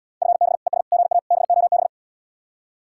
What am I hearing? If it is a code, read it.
Morse code, heard as a keyed tone, about 77 wpm, 700 Hz. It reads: FB ES GA BOB